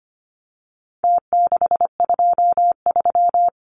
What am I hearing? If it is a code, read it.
Morse code, heard as a keyed tone, about 25 wpm, 700 Hz. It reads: T623
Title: Numbers - Contest Style - 4-Digit 25wpm